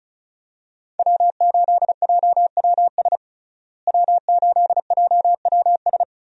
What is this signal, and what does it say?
Morse code, heard as a keyed tone, about 34 wpm, 700 Hz. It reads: W8JWS W8JWS
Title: USA Call Signs - 5 Characters - Repeated 2x - 35wpm